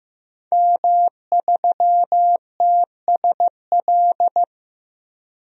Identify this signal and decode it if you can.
Morse code, heard as a keyed tone, about 15 wpm, 700 Hz. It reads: M3TSL
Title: DXCC-100 Call Signs - 5 Characters - 15wpm